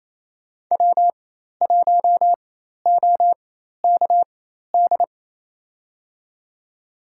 Morse code, heard as a keyed tone, about 28 wpm, spaced as for 13 wpm, 700 Hz. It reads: W1OKD
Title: USA Call Signs - 4x Character Spacing - 28wpm